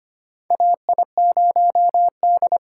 Morse code, heard as a keyed tone, about 25 wpm, 700 Hz. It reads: AI0D
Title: USA Call Signs - 4 Characters - 3x Character Spacing - 25wpm